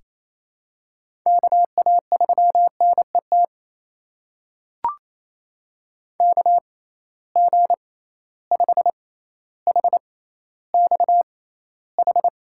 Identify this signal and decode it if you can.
Morse code, heard as a keyed tone, about 28 wpm, spaced as for 14 wpm, 700 Hz. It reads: KA3NET KG5HXH
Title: USA Call Signs - 6 Characters - 6x Character Spacing - 28wpm